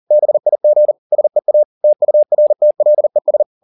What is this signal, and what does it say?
Morse code, heard as a keyed tone, about 40 wpm, 600 Hz. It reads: BIG SEA TURTLES